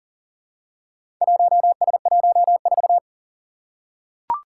Morse code, heard as a keyed tone, about 40 wpm, 700 Hz. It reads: 1S14
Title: Four Challenging Letter-Number Combinations 40wpm